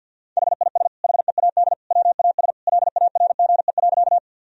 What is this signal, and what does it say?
Morse code, heard as a keyed tone, about 51 wpm, 700 Hz. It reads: HIS HEAD WAS LARGE.
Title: Sentences from Top 1000 Words - Review Part 1 - 50wpm